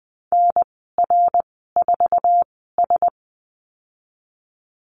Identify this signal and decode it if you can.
Morse code, heard as a keyed tone, about 20 wpm, spaced as for 15 wpm, 700 Hz. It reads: NR4S
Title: 055 Lesson 10g - Letter r - Call Signs 20wpm